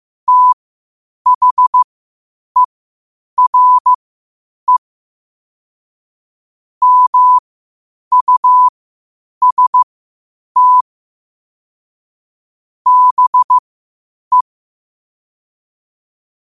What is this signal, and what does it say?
Morse code, heard as a keyed tone, about 15 wpm, spaced as for 8 wpm, 1000 Hz. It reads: THERE MUST BE
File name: qotdfiles / 8wpm